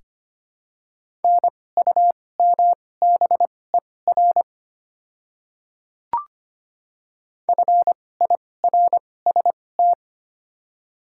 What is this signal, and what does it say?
Morse code, heard as a keyed tone, about 25 wpm, spaced as for 18 wpm, 700 Hz. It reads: NUMBER FIRST
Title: 2x Character Spacing - Top 300 Words - 25wpm